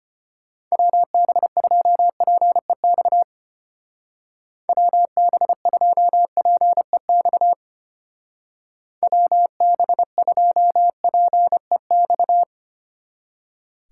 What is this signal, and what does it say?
Morse code, heard as a keyed tone, about 31 wpm, 700 Hz. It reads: WB2PEX WB2PEX WB2PEX
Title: Speed Racing - USA 2x3 Call Signs 25wpm